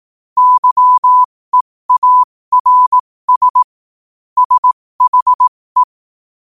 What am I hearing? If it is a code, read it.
Morse code, heard as a keyed tone, about 18 wpm, spaced as for 15 wpm, 1000 Hz. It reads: YEARS SHE
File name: 15wpm